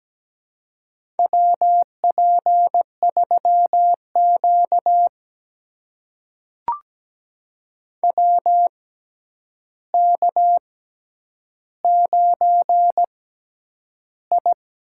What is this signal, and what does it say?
Morse code, heard as a keyed tone, about 17 wpm, spaced as for 9 wpm, 700 Hz. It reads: WP3Q WK9I
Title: USA Call Signs - 4 Characters - 6x Character Spacing - 17wpm